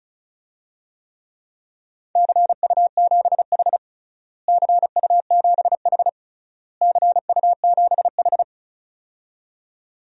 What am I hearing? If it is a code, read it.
Morse code, heard as a keyed tone, about 36 wpm, 700 Hz. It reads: CU7H CU7H CU7H
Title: DXCC-100 Call Signs - 4 Characters - Repeated 3x - 35wpm